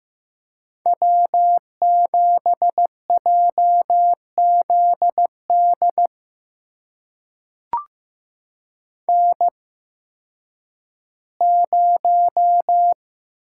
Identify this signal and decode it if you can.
Morse code, heard as a keyed tone, about 15 wpm, spaced as for 10 wpm, 700 Hz. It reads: W7JZD N0
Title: USA Call Signs - 5 Characters - 8x Character Spacing - 15wpm